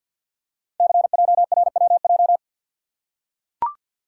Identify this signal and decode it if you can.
Morse code, heard as a keyed tone, about 45 wpm, 690 Hz. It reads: CJRWJ
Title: Code Groups - 5 Letters - 50wpm